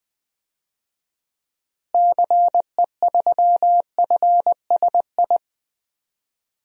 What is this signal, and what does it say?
Morse code, heard as a keyed tone, about 20 wpm, 700 Hz. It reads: CE3FSI
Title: DXCC-100 Call Signs - 6 Characters - Repeated 2x - 20wpm